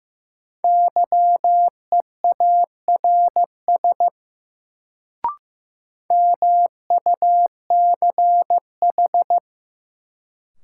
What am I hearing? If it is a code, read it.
Morse code, heard as a keyed tone, about 15 wpm, 700 Hz. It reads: YEARS MUCH
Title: Sentences from Top 200 Words - New Word Drill - 15wpm